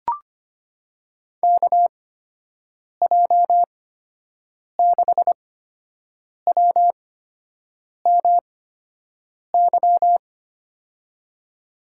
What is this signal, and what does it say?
Morse code, heard as a keyed tone, about 25 wpm, spaced as for 7 wpm, 700 Hz. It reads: KJ6WMY
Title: USA Call Signs - 6 Characters - 8x Character Spacing - 25wpm